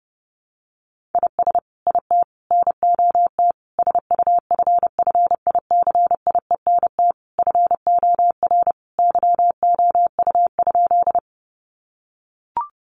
Morse code, heard as a keyed tone, about 30 wpm, 700 Hz. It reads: IS IT NOT SUFFICIENT FOR YOU?
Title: Sentences from Top 1400 Words 30wpm